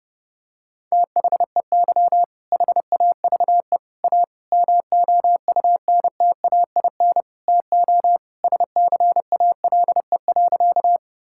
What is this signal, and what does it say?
Morse code, heard as a keyed tone, about 30 wpm, 700 Hz. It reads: THEY HAVE A MOUNTAIN TO SCALE.
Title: Sentences from Top 1000 Words - Review Part 5 - 30wpm